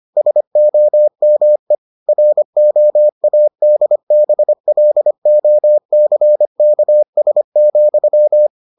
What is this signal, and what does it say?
Morse code, heard as a keyed tone, about 25 wpm, 600 Hz. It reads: SOME ROADBLOCKS,